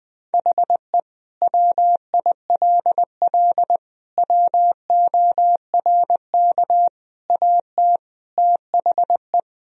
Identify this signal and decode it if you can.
Morse code, heard as a keyed tone, about 19 wpm, 700 Hz. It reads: HE WILL WORK AT THE
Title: Sentences from Top 1000 Words - Review Part 1 - 20wpm